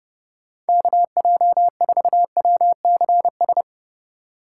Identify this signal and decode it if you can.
Morse code, heard as a keyed tone, about 39 wpm, 700 Hz. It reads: KJ4WCS